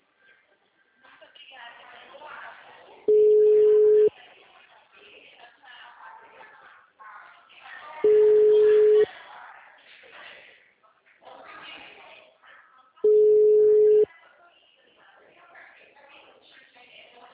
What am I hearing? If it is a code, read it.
Ringback tone (Europe, 425 Hz), with speech.
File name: out- -2018- - - .398